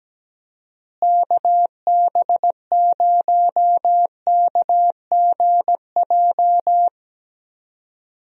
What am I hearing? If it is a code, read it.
Morse code, heard as a keyed tone, about 17 wpm, 700 Hz. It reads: KB0KGJ
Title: USA Call Signs - 6 Characters - Repeated 3x - 17wpm